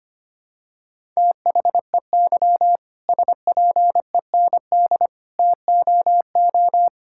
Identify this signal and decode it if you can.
Morse code, heard as a keyed tone, about 25 wpm, 700 Hz. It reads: THEY SPEND TOO
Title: Sentences from Top 1000 Words - Review Part 5 - 25wpm